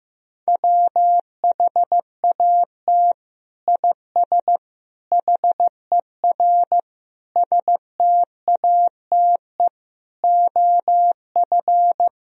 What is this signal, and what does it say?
Morse code, heard as a keyed tone, about 15 wpm, 700 Hz. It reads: WHAT IS HER STATE OF
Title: Sentences from Top 1000 Words - Review Part 2 - 15wpm